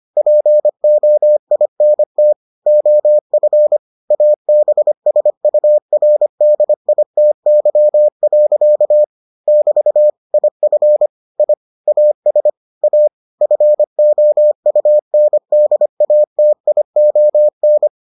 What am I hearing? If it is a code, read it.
Morse code, heard as a keyed tone, about 25 wpm, 600 Hz. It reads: POINT OF ABSURDITY. <BT> IF I AS A FOUNDATION